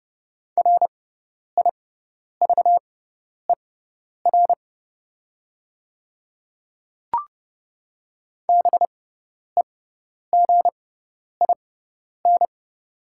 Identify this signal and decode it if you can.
Morse code, heard as a keyed tone, about 30 wpm, spaced as for 10 wpm, 700 Hz. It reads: RIVER BEGIN
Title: 6x Character Spacing - Top 300 Words - 30wpm